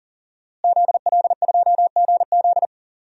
Morse code, heard as a keyed tone, about 40 wpm, 700 Hz. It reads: ZL2GZ
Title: DXCC-100 Call Signs - 5 Characters - Repeated 3x - 40wpm